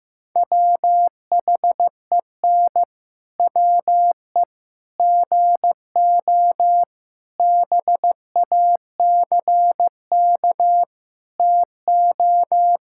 Morse code, heard as a keyed tone, about 14 wpm, 700 Hz. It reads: WHEN WE GO BACK TO
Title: Pangram Challenge v2 - 15wpm